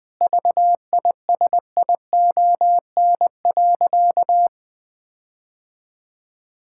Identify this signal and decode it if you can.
Morse code, heard as a keyed tone, about 20 wpm, 700 Hz. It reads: VISION.
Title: Difficult Sentences with Hints - 20wpm